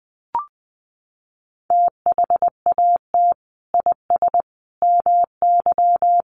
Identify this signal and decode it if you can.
Morse code, heard as a keyed tone, about 20 wpm, 700 Hz. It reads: THAT IS MY
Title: Sentences from Top 700 Words 20wpm